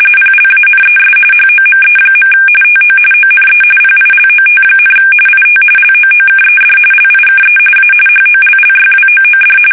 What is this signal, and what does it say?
RTTY, about 45.45 baud. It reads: PLANES HOME. TO ONE PILOT, IT LOOKED LIKE A HOLLYWOOD
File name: fsk850 / battleshipnj